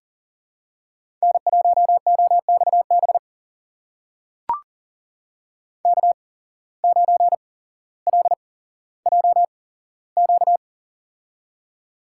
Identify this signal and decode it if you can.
Morse code, heard as a keyed tone, about 37 wpm, spaced as for 17 wpm, 700 Hz. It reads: N1OXB K9LJQ
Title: USA Call Signs - 5 Characters - 8x Character Spacing - 40wpm